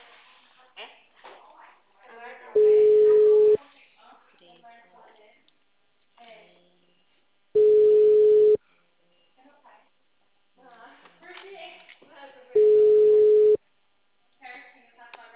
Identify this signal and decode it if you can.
Ringback tone (Europe, 425 Hz), with speech.